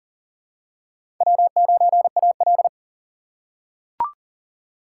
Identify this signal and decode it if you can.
Morse code, heard as a keyed tone, about 40 wpm, 700 Hz. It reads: W9AL
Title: USA Call Signs - 3x Character Spacing - 40wpm